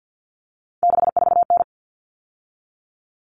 Morse code, heard as a keyed tone, about 50 wpm, 700 Hz. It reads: 64N
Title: All Cut Numbers - 3-Digit 50wpm